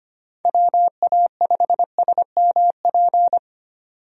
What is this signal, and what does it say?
Morse code, heard as a keyed tone, about 25 wpm, 700 Hz. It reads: WA5SMP
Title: USA Call Signs - 6 Characters - 25wpm